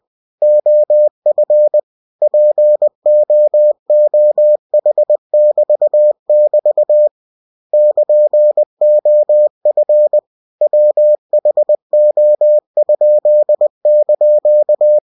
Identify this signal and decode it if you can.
Morse code, heard as a keyed tone, about 20 wpm, 600 Hz. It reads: OF POOH<BT><BT> <KN>OF WHO?)
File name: cw-books / winnie-the-pooh / winnie-20wpm-page-0103